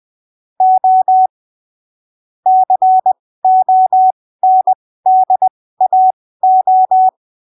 Morse code, heard as a keyed tone, about 20 wpm, spaced as for 15 wpm, 750 Hz. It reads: O CONDAO
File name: 20ppm